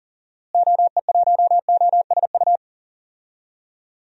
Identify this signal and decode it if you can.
Morse code, heard as a keyed tone, about 40 wpm, 700 Hz. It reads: OE1OSU
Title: DXCC-100 Call Signs - 6 Characters - Repeated 2x - 40wpm